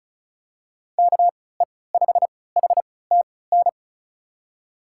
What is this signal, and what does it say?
Morse code, heard as a keyed tone, about 34 wpm, spaced as for 20 wpm, 700 Hz. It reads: KE5HTN
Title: USA Call Signs - 6 Characters - 3x Character Spacing - 35wpm